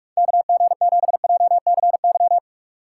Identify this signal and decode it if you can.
Morse code, heard as a keyed tone, about 56 wpm, 700 Hz. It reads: KG7JCY
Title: Speed Racing - USA Call Signs 45wpm